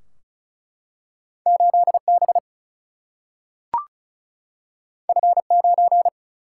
Morse code, heard as a keyed tone, about 36 wpm, 700 Hz. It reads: 8 B F 9
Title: Single-Letter Single-Number 35wpm